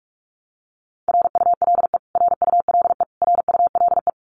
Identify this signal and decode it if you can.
Morse code, heard as a keyed tone, about 44 wpm, 700 Hz. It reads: RULE RULE RULE
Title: Top 500 Words Repeated 45wpm